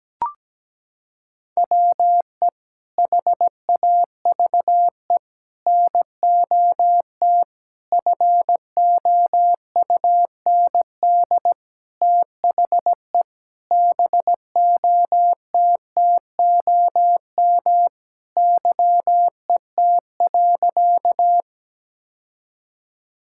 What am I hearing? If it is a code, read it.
Morse code, heard as a keyed tone, about 17 wpm, 700 Hz. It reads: WE HAVE NOT FOUND THE BOTTOM YET.
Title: Sentences from Top 1000 Words - Review Part 6 - 17wpm